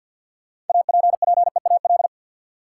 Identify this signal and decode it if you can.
Morse code, heard as a keyed tone, about 48 wpm, 700 Hz. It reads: APPEAL
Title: Sentences from Top 1300 Words - New Word Drill - 50wpm